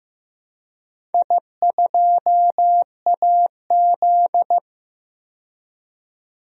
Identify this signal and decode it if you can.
Morse code, heard as a keyed tone, about 15 wpm, 700 Hz. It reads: I2AZ
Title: DXCC-100 Call Signs - 4 Characters - 15wpm